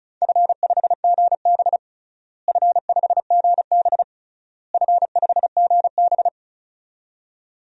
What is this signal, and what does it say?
Morse code, heard as a keyed tone, about 36 wpm, 700 Hz. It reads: F5GB F5GB F5GB
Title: DXCC-100 Call Signs - 4 Characters - Repeated 3x - 35wpm